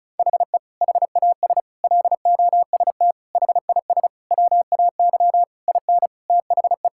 Morse code, heard as a keyed tone, about 36 wpm, 700 Hz. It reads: HE HAS LOST HIS WAY IN THE
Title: Sentences from Top 1600 Words 35wpm